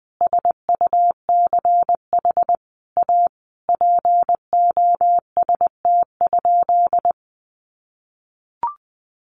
Morse code, heard as a keyed tone, about 20 wpm, 700 Hz. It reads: SUCH A POST?
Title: Sentences from Top 1000 Words 20wpm